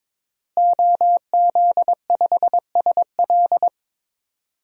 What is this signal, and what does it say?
Morse code, heard as a keyed tone, about 22 wpm, 700 Hz. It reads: OZ5SL
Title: DXCC-100 Call Signs - 5 Characters - Repeated 2x - 22wpm